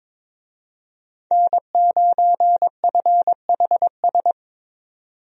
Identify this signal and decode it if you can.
Morse code, heard as a keyed tone, about 22 wpm, 700 Hz. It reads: N9FHS
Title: USA Call Signs - 5 Characters - 22wpm